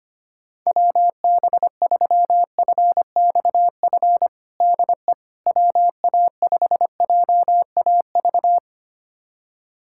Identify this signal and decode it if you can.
Morse code, heard as a keyed tone, about 25 wpm, 700 Hz. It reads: WB3FXF DE WA5JAV
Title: USA Callsign DE Callsign 25wpm